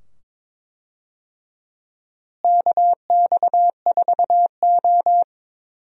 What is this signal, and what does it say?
Morse code, heard as a keyed tone, about 22 wpm, 700 Hz. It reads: KX4O
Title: USA Call Signs - 4 Characters - Repeated 2x - 22wpm